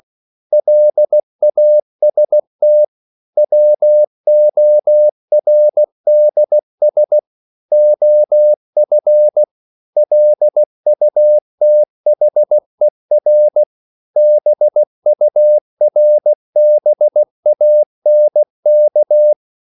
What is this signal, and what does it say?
Morse code, heard as a keyed tone, about 16 wpm, 600 Hz. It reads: LAST WORDS OF LUTHER BURBANK